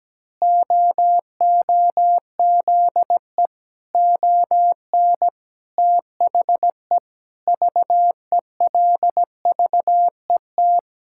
Morse code, heard as a keyed tone, about 17 wpm, 700 Hz. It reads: OOZE ON THE VELVET
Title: Pangram Challenge v2 - 17wpm